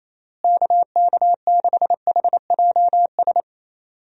Morse code, heard as a keyed tone, about 28 wpm, 700 Hz. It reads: KK6HJS
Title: USA Call Signs - 6 Characters - Repeated 2x - 28wpm